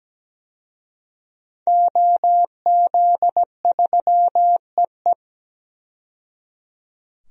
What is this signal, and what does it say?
Morse code, heard as a keyed tone, about 17 wpm, 700 Hz. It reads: OZ3EE
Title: DXCC-100 Call Signs - 5 Characters - 17wpm